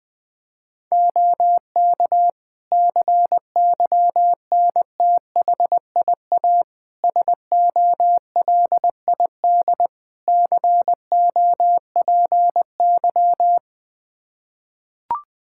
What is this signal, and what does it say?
Morse code, heard as a keyed tone, about 20 wpm, 700 Hz. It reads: OK CYNTHIA SOLID COPY
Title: K1USN SST QSO Practice 20wpm